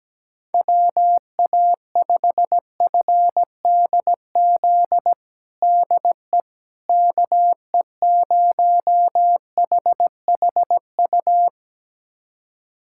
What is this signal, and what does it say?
Morse code, heard as a keyed tone, about 17 wpm, 700 Hz. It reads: WA5FDZ DE KE0HHU
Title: USA Callsign DE Callsign 17wpm